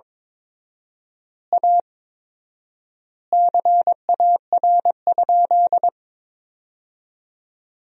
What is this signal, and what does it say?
Morse code, heard as a keyed tone, about 22 wpm, 700 Hz. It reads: A CAR?
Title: 4x Word-Spacing - Sentences from Top 300 Words - 22wpm